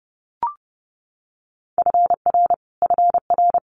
Morse code, heard as a keyed tone, about 30 wpm, 700 Hz. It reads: FR FR
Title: Multi-letter Phonemes Repeated 3x 30wpm